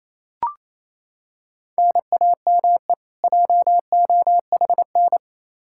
Morse code, heard as a keyed tone, about 27 wpm, 700 Hz. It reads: NAME JOHN
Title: Practice QSO - Exchange 1 - Beginning - 28wpm